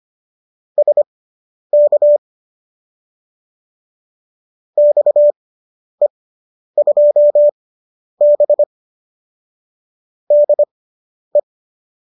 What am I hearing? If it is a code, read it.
Morse code, heard as a keyed tone, about 25 wpm, spaced as for 10 wpm, 590 Hz. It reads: SK XE2B DE